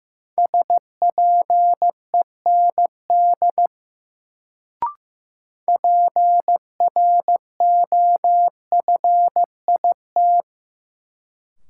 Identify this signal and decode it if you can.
Morse code, heard as a keyed tone, about 15 wpm, 700 Hz. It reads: SPEND PROFIT
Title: Sentences from Top 800 Words - New Word Drill - 15wpm